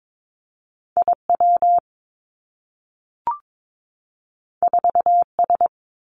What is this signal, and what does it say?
Morse code, heard as a keyed tone, about 22 wpm, 700 Hz. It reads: I W 4 S
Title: Two Challenging Letter-Number Combinations 22wpm